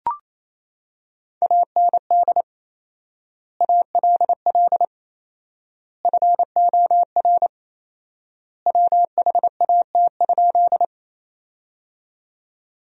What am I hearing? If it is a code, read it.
Morse code, heard as a keyed tone, about 28 wpm, 700 Hz. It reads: AND ALL FOR WHAT?
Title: 4x Word-Spacing - Sentences from Top 100 Words - 28wpm